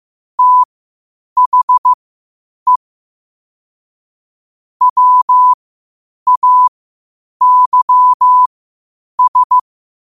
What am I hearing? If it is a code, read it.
Morse code, heard as a keyed tone, about 15 wpm, spaced as for 8 wpm, 1000 Hz. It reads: THE WAYS
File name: qotdfiles / 8wpm